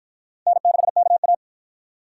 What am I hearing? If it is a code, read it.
Morse code, heard as a keyed tone, about 52 wpm, 700 Hz. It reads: N6KA